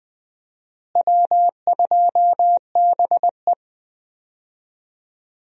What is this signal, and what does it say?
Morse code, heard as a keyed tone, about 20 wpm, 700 Hz. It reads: W2BE
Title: USA Call Signs - 4 Characters - Repeated 2x - 20wpm